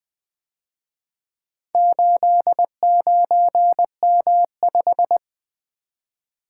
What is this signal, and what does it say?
Morse code, heard as a keyed tone, about 24 wpm, 700 Hz. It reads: 89M5
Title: Speed-Racing - Four Letter-Number Combinations - 20wpm